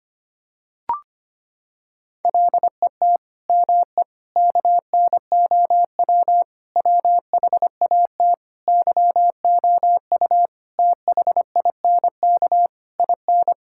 Morse code, heard as a keyed tone, about 25 wpm, 700 Hz. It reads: LET ME KNOW WHAT YOU THINK IN
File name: Sets of 9 Words - 25wpm